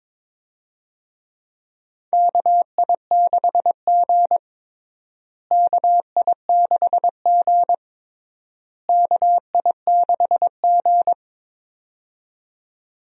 Morse code, heard as a keyed tone, about 22 wpm, 700 Hz. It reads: KI6G KI6G KI6G
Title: USA Call Signs - 4 Characters - Repeated 3x - 22wpm